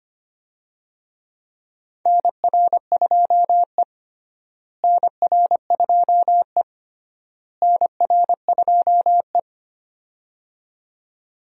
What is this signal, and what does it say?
Morse code, heard as a keyed tone, about 25 wpm, 700 Hz. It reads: NR2E NR2E NR2E
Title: USA Call Signs - 4 Characters - Repeated 3x - 25wpm